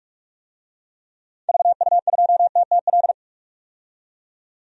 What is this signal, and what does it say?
Morse code, heard as a keyed tone, about 44 wpm, 690 Hz. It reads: VU2TTL
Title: DXCC-100 Call Signs - 6 Characters - 45wpm